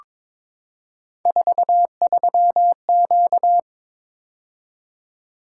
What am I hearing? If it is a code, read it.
Morse code, heard as a keyed tone, about 22 wpm, 700 Hz. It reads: 43Q
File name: practice / 22wpm / Three Letter-Number Combinations - 22wpm